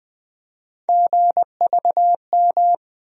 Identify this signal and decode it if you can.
Morse code, heard as a keyed tone, about 20 wpm, 700 Hz. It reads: GVM
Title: USA Call Sign Postfixes - 3 Letters 20wpm